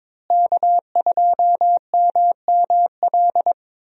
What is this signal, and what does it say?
Morse code, heard as a keyed tone, about 22 wpm, 700 Hz. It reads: K2MML
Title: USA Call Signs - 5 Characters - Repeated 3x - 22wpm